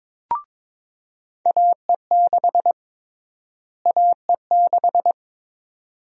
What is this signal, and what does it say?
Morse code, heard as a keyed tone, about 22 wpm, 700 Hz. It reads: AE6 AE6
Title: USA Call Sign Prefixes - Repeated 2x - 22wpm